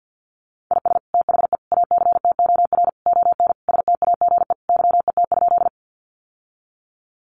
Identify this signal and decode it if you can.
Morse code, heard as a keyed tone, about 50 wpm, 700 Hz. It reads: IS THE ACTOR ON STAGE YET?
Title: Sentences from Top 1500 Words 50wpm